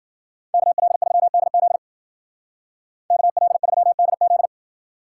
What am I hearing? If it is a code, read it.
Morse code, heard as a keyed tone, about 53 wpm, 700 Hz. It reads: KL3DZ KL3DZ